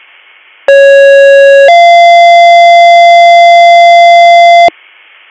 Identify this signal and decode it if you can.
Two-tone page, heard as a one-second tone, then a three-second tone, about 550 then 690 Hz, with noise. It is heard from 0.7 to 4.7 s.